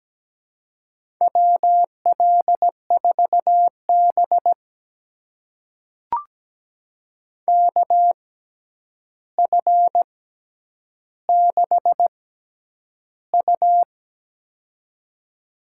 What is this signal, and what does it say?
Morse code, heard as a keyed tone, about 17 wpm, spaced as for 9 wpm, 700 Hz. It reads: WL4B KF6U
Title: USA Call Signs - 6x Character Spacing - 17wpm